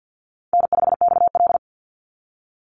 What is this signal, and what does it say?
Morse code, heard as a keyed tone, about 50 wpm, 700 Hz. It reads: N5XL
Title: NA SOTA Chaser Callsigns 50wpm